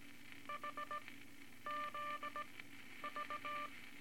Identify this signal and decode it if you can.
Morse code, heard as a keyed tone, about 18 wpm, 1180 Hz, with noise. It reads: H Z V
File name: Morse Code Records / 18 A LCM 164 Ten Words Per Minute